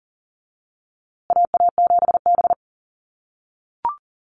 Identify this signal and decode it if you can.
Morse code, heard as a keyed tone, about 37 wpm, 700 Hz. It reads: AA7B
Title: USA Call Signs Repeated 2x 40wpm